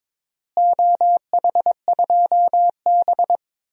Morse code, heard as a keyed tone, about 22 wpm, 700 Hz. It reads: OH2B
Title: DXCC-100 Call Signs - 4 Characters - Repeated 3x - 22wpm